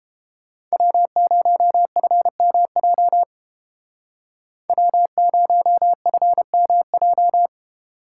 Morse code, heard as a keyed tone, about 31 wpm, 700 Hz. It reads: W0FMJ W0FMJ